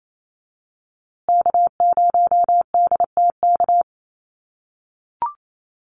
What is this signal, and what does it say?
Morse code, heard as a keyed tone, about 28 wpm, 700 Hz. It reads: K0DTK
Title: USA Call Signs Repeated 3x 28wpm